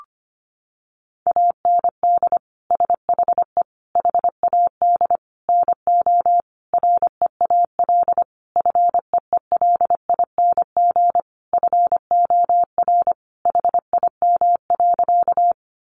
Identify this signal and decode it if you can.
Morse code, heard as a keyed tone, about 25 wpm, 700 Hz. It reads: AND SHE HAD NO REAL FEELING FOR HIM.
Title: Sentences from Top 1200 Words 25wpm